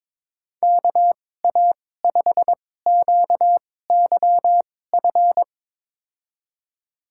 Morse code, heard as a keyed tone, about 22 wpm, spaced as for 16 wpm, 700 Hz. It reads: KA5QYF
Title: USA Call Signs - 6 Characters - 2x Character Spacing - 22wpm